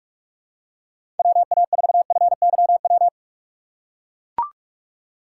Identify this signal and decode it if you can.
Morse code, heard as a keyed tone, about 43 wpm, 700 Hz. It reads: WA4FYW
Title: USA Call Signs - 6 Characters - 2x Character Spacing - 45wpm